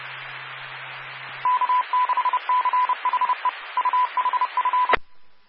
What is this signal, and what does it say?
Morse code, heard as a keyed tone, about 30 wpm, 1000 Hz. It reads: K6CHE UHF
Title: Claremont, CA Police, Amateur Radio